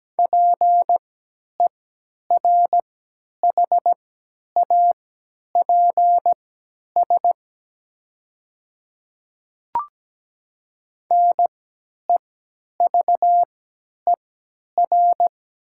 Morse code, heard as a keyed tone, about 17 wpm, spaced as for 10 wpm, 700 Hz. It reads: PERHAPS NEVER
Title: 3x Character Spacing - Top 500 Words - 17wpm